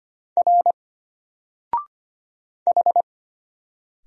Morse code, heard as a keyed tone, about 25 wpm, 700 Hz. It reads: R H
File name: Single-Letter-Number-25wpm